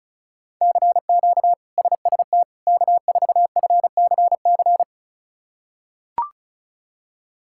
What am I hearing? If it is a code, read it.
Morse code, heard as a keyed tone, about 36 wpm, 700 Hz. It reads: CQ SST K4FCC
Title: K1USN SST QSO Practice - 35wpm